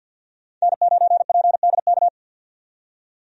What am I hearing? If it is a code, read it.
Morse code, heard as a keyed tone, about 52 wpm, 700 Hz. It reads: N9PDK
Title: USA Call Signs - 5 Characters - 50wpm